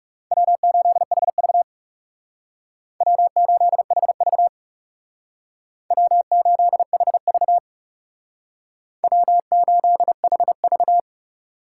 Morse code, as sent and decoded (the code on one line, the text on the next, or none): .-- ---.. .... ...- / .-- ---.. .... ...- / .-- ---.. .... ...- / .-- ---.. .... ...-
W8HV W8HV W8HV W8HV